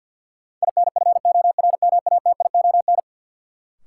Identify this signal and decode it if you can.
Morse code, heard as a keyed tone, about 52 wpm, 700 Hz. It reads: INFORMATION
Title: Sentences from Top 600 Words - New Word Drill - 50wpm